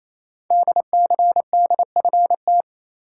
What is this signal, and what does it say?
Morse code, heard as a keyed tone, about 29 wpm, 700 Hz. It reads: DCDFT
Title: Code Groups - 5 Letters - 28wpm